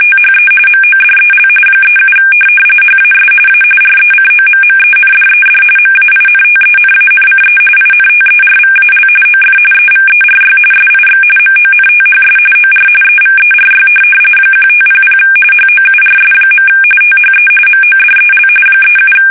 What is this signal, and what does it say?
RTTY, about 45.45 baud. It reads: THE MODEL 37 HAS A RECOMMENDED MAINTENANCE INTERVAL OF EVERY SIX MONTHS OR EVERY 1500 HOURS. THE MODEL 37
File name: fsk850 / teletype-corp